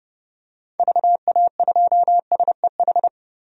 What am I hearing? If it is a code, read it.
Morse code, heard as a keyed tone, about 30 wpm, 700 Hz. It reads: VA2SEH
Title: DXCC-100 Call Signs - 6 Characters - 30wpm